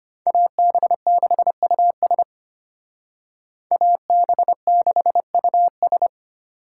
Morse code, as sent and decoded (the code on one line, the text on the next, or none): .- -... -.... ..- ... / .- -... -.... ..- ...
AB6US AB6US